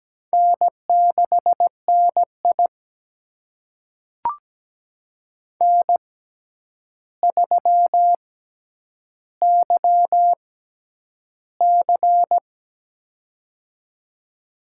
Morse code, heard as a keyed tone, about 17 wpm, spaced as for 9 wpm, 700 Hz. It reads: N6NI N3YC
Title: USA Call Signs - 4 Characters - 6x Character Spacing - 17wpm